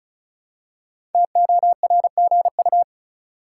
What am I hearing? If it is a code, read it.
Morse code, heard as a keyed tone, about 36 wpm, 700 Hz. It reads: TORGU
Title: Code Groups - 5 Letters - 35wpm